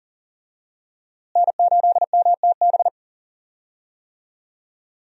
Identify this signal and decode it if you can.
Morse code, heard as a keyed tone, about 39 wpm, 700 Hz. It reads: N8MTB
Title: USA Call Signs - 5 Characters - Repeated 3x - 40wpm